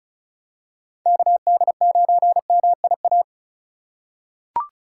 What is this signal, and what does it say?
Morse code, heard as a keyed tone, about 36 wpm, 700 Hz. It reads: KD9MIA
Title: USA Call Signs - 4x Character Spacing - 35wpm